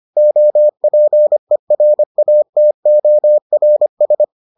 Morse code, heard as a keyed tone, about 25 wpm, 600 Hz. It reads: OPERATORS